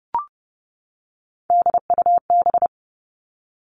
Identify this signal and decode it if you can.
Morse code, heard as a keyed tone, about 30 wpm, 700 Hz. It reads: DUB
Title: Three Challenging Letter-Number Combinations 30wpm